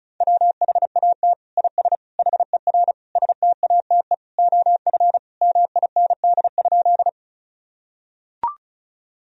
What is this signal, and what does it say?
Morse code, heard as a keyed tone, about 34 wpm, 700 Hz. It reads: WHAT IS HER STATE OF MIND?
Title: Sentences from Top 1000 Words - Review Part 2 - 35wpm